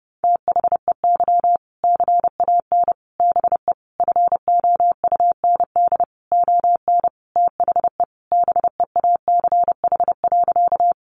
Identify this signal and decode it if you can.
Morse code, heard as a keyed tone, about 30 wpm, 700 Hz. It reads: THEY CAN BE FOUND ON THE BEACH.
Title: Sentences from Top 1500 Words 30wpm